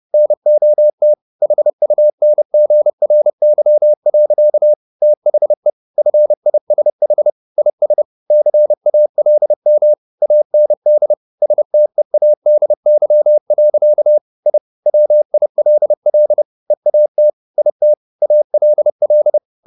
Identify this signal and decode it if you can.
Morse code, heard as a keyed tone, about 30 wpm, 600 Hz. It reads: NOT HUNGRY. THE FISH IS CALM AND STEADY. I WILL EAT IT ALL
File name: the old man and the sea / the old man and the sea 1 30wpm